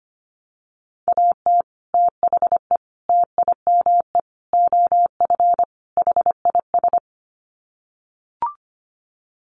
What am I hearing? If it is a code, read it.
Morse code, heard as a keyed tone, about 25 wpm, 700 Hz. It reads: AT THE TIME OF HIS
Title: Sets of 5 Words Encore2 25wpm